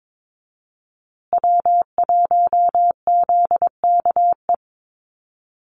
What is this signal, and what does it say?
Morse code, heard as a keyed tone, about 22 wpm, 700 Hz. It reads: W1ZKE
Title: USA Call Signs Repeated 3x 22wpm